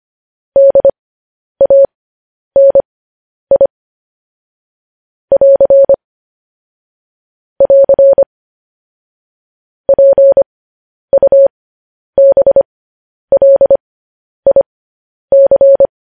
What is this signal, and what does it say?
Morse code, heard as a keyed tone, about 25 wpm, spaced as for 10 wpm, 550 Hz. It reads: DANI <AR> <AR> PUBLIC
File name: podcast / News headlines-1x-25x10